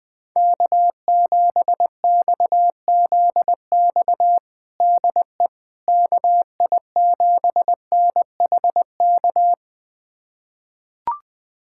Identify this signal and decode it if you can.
Morse code, heard as a keyed tone, about 20 wpm, 700 Hz. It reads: K7XZX DE KI7NHK
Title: USA Callsign DE Callsign 20wpm